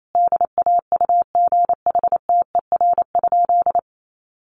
Morse code, heard as a keyed tone, about 28 wpm, 700 Hz. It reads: DAUGHTER?
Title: Sentences from Top 1500 Words 28wpm